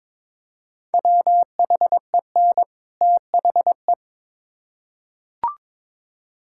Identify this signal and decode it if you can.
Morse code, heard as a keyed tone, about 22 wpm, 700 Hz. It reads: WHEN THE
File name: Build A Phrase - 2-Word Phrases - 22wpm